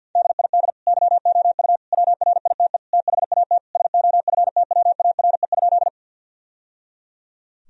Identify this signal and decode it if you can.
Morse code, heard as a keyed tone, about 52 wpm, 700 Hz. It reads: DID YOU WRITE THAT SOFTWARE?
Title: Sentences from Top 1000 Words - Review Part 3 - 50wpm